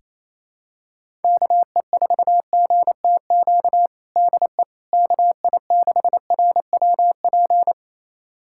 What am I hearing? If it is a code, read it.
Morse code, heard as a keyed tone, about 28 wpm, 700 Hz. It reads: KE4GTQ DE KI6RWP
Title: USA Callsign DE Callsign - 6 Characters - 28wpm